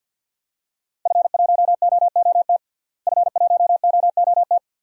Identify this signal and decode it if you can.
Morse code, heard as a keyed tone, about 47 wpm, 700 Hz. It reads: F1OOT F1OOT
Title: DXCC-100 Call Signs - 5 Characters - Repeated 2x - 50wpm